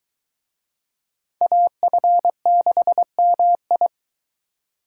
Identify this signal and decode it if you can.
Morse code, heard as a keyed tone, about 22 wpm, 700 Hz. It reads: AF6MI